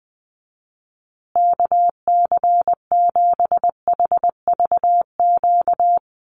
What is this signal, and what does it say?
Morse code, heard as a keyed tone, about 20 wpm, 700 Hz. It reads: KC7HVQ
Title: USA Call Signs Repeated 3x 20wpm